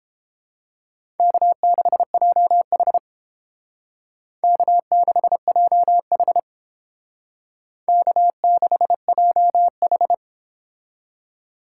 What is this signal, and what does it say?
Morse code, heard as a keyed tone, about 29 wpm, 700 Hz. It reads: K6JH K6JH K6JH